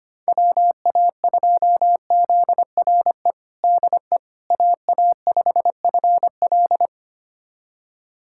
Morse code, heard as a keyed tone, about 25 wpm, 700 Hz. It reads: WA2ZRE DE AA5FL
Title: USA Callsign DE Callsign 25wpm